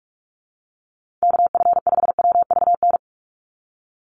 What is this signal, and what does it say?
Morse code, heard as a keyed tone, about 46 wpm, 690 Hz. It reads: KF5WVN
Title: USA Call Signs 45wpm